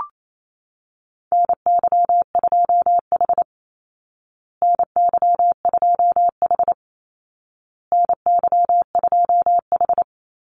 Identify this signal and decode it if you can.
Morse code, heard as a keyed tone, about 28 wpm, 700 Hz. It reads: NY2H NY2H NY2H
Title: Call Signs Repeated 2x - 28wpm